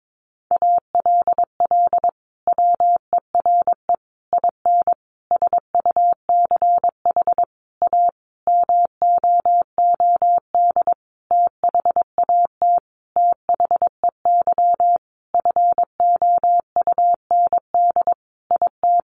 Morse code, heard as a keyed tone, about 22 wpm, 700 Hz. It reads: ALL WERE IN SUCH A MOOD THAT THEY FOUND IT
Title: Sentences from Top 1100 Words 22wpm